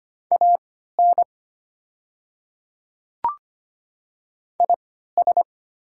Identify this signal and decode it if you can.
Morse code, heard as a keyed tone, about 25 wpm, spaced as for 14 wpm, 700 Hz. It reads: AN IS
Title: 3x Character Spacing - Top 100 Words - 25wpm